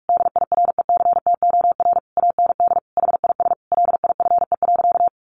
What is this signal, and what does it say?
Morse code, heard as a keyed tone, about 44 wpm, 700 Hz. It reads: DIRECTOR AND HIS LIFE.
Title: Sentences from Top 1300 Words 45wpm